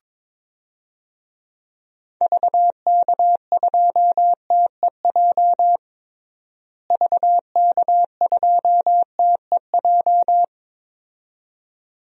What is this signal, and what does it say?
Morse code, heard as a keyed tone, about 22 wpm, 700 Hz. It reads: VK2TEJ VK2TEJ
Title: DXCC-100 Call Signs - 6 Characters - Repeated 2x - 22wpm